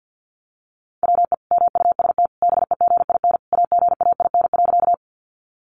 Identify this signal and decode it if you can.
Morse code, heard as a keyed tone, about 55 wpm, 700 Hz. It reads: WE MUST BEGIN AGAIN.
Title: Sentences from Top 300 Words 50wpm